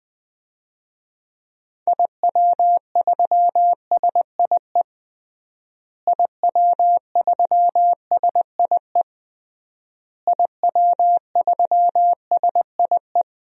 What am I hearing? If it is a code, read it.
Morse code, heard as a keyed tone, about 21 wpm, 700 Hz. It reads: IW3SIE IW3SIE IW3SIE
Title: DXCC-100 Call Signs - 6 Characters - Repeated 3x - 20wpm